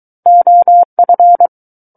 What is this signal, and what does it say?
Morse code, heard as a keyed tone, about 23 wpm, 700 Hz. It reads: OF
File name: Book One / 14-Chapter 14 - 23 WPM